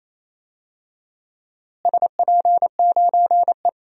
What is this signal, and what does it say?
Morse code, heard as a keyed tone, about 28 wpm, 700 Hz. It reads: SP9E
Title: DXCC-100 Call Signs - 4 Characters - Repeated 3x - 28wpm